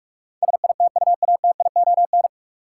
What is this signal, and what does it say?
Morse code, heard as a keyed tone, about 41 wpm, 700 Hz. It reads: SITUATION
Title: Sentences from Top 1200 Words - New Word Drill - 45wpm